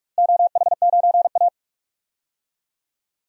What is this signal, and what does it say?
Morse code, heard as a keyed tone, about 46 wpm, 700 Hz. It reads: OH9A
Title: DXCC-100 Call Signs - 4 Characters - Repeated 2x - 45wpm